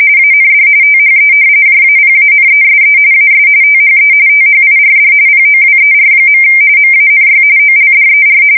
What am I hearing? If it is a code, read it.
RTTY, about 45.45 baud. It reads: AND THE JAPANESE AS THE THIRD POWER. MANY OLDER
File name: fsk170 / usnavy4